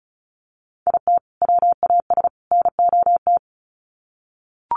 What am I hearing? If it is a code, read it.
Morse code, heard as a keyed tone, about 34 wpm, 700 Hz. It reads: IT WAS NOT
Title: Sets of 3 Words 35wpm